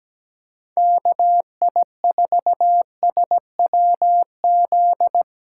Morse code, heard as a keyed tone, about 17 wpm, 700 Hz. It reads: KI4SWZ
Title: USA Callsign DE Callsign 17wpm